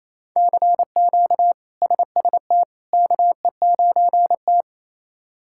Morse code, heard as a keyed tone, about 28 wpm, 700 Hz. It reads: CQ SST KE9T
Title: K1USN SST QSO Practice - 28wpm